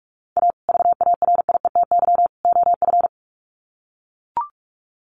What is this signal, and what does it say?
Morse code, heard as a keyed tone, about 44 wpm, 700 Hz. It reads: A VARIETY OF
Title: Sets of 3 Words 45wpm